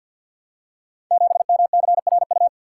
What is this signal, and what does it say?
Morse code, heard as a keyed tone, about 48 wpm, 700 Hz. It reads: 7MCRU
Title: Code Groups - 5 Letter:Numbers - 50wpm